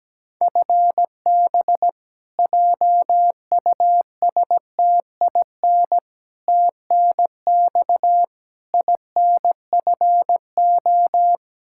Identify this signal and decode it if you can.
Morse code, heard as a keyed tone, about 17 wpm, 700 Hz. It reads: FB JUSTIN TNX INFO
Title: Practice QSO - Exchange 3 - More Info - 17wpm